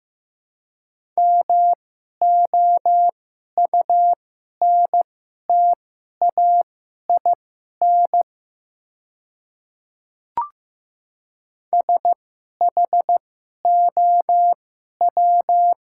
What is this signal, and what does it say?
Morse code, heard as a keyed tone, about 15 wpm, spaced as for 11 wpm, 700 Hz. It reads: MOUNTAIN SHOW
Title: 2x Character Spacing - Top 500 Words - 15wpm